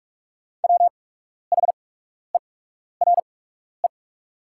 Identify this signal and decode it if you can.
Morse code, heard as a keyed tone, about 44 wpm, spaced as for 12 wpm, 690 Hz. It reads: WHERE
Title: 8x Character Spacing - Top 200 Words - 45wpm